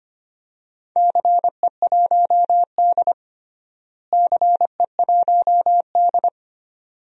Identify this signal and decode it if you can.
Morse code, heard as a keyed tone, about 25 wpm, 700 Hz. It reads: CE1D CE1D
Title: DXCC-100 Call Signs - 4 Characters - Repeated 2x - 25wpm